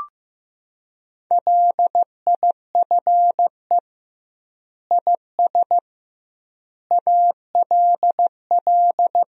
Morse code, heard as a keyed tone, about 15 wpm, 700 Hz. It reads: LIFE IS ALL
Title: 2x Word-Spacing - Sentences from Top 300 Words - 15wpm